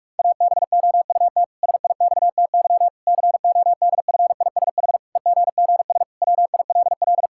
Morse code, heard as a keyed tone, about 46 wpm, 700 Hz. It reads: ABOUT SIXTY CODFISH EGGS WILL
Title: Pangram Challenge v2 - 45wpm